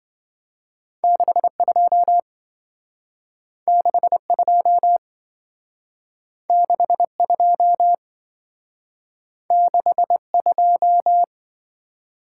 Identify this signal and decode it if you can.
Morse code, heard as a keyed tone, about 26 wpm, 700 Hz. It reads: 62 62 62 62